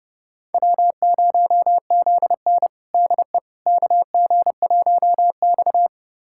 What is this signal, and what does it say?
Morse code, heard as a keyed tone, about 31 wpm, 700 Hz. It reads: W0ZN DE KG1X
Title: USA Callsign DE Callsign - 4 Characters - 30wpm